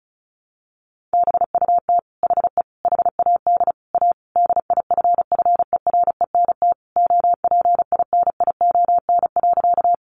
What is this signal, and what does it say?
Morse code, heard as a keyed tone, about 34 wpm, 700 Hz. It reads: BUT HE HAD A DIFFERENT OPINION.
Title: Sentences from Top 1200 Words 35wpm